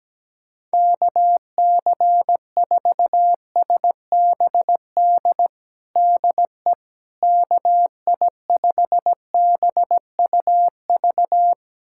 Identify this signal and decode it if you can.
Morse code, heard as a keyed tone, about 17 wpm, 700 Hz. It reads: KC4SBD DE KI5BUV
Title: USA Callsign DE Callsign - 6 Characters - 17wpm